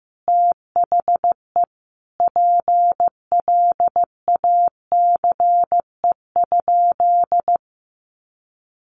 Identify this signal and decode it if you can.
Morse code, heard as a keyed tone, about 15 wpm, 700 Hz. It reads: THE PLACE?
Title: Sentences from Top 200 Words 15wpm